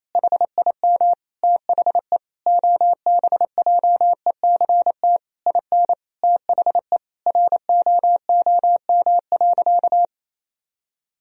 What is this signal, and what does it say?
Morse code, heard as a keyed tone, about 28 wpm, 700 Hz. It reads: HIM THE OBJECT IN THE ROOM.
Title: Sentences from Top 1000 Words - Review Part 3 - 28wpm